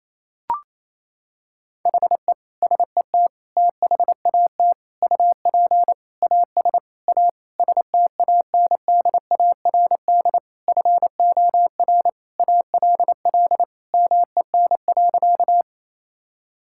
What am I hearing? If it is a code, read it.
Morse code, heard as a keyed tone, about 28 wpm, 700 Hz. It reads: HE SET THAT UP AS A STANDARD FOR ALL MEN.
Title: Sentences from Top 1000 Words - Review Part 6 - 28wpm